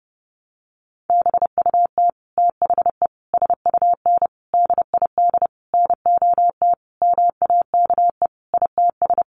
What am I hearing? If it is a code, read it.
Morse code, heard as a keyed tone, about 28 wpm, 700 Hz. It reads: BUT THE SUN DID NOT MAKE ITS
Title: Sentences from Top 1100 Words 30wpm